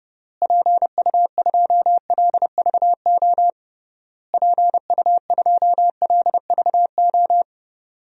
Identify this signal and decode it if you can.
Morse code, heard as a keyed tone, about 30 wpm, 700 Hz. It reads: PU2LVO PU2LVO
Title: DXCC-100 Call Signs - 6 Characters - Repeated 3x - 30wpm